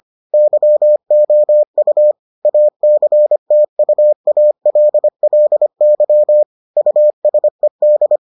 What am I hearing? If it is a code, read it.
Morse code, heard as a keyed tone, about 25 wpm, 600 Hz. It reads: YOU ACTUALLY USED